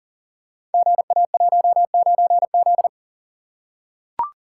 Morse code, heard as a keyed tone, about 38 wpm, 690 Hz. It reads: GA19Z
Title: Code Groups - 5 Letter:Numbers - 40wpm